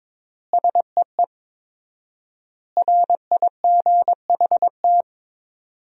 Morse code, heard as a keyed tone, about 23 wpm, 700 Hz. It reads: SEE RIGHT
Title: 4x Word-Spacing - Sentences from Top 200 Words - 22wpm